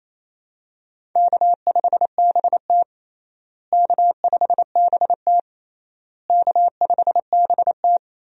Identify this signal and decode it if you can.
Morse code, heard as a keyed tone, about 27 wpm, 700 Hz. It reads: K5BT K5BT K5BT
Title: USA Call Signs - 4 Characters - Repeated 3x - 28wpm